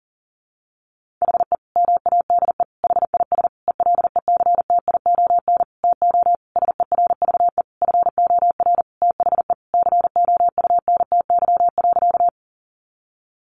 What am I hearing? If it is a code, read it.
Morse code, heard as a keyed tone, about 40 wpm, 700 Hz. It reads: HE MADE HIS ELECTION TO SERVE FOR THE COUNTY.
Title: Sentences from Top 1500 Words 40wpm